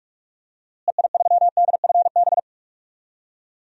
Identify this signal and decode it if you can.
Morse code, heard as a keyed tone, about 41 wpm, 700 Hz. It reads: EI3DFB
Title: DXCC-100 Call Signs - 6 Characters - Repeated 3x - 45wpm